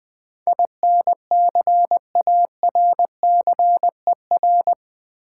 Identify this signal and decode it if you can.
Morse code, heard as a keyed tone, about 20 wpm, 700 Hz. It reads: INCARCER
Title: Build A Word - 4 Letters At A Time - 11-Letter Words - 20wpm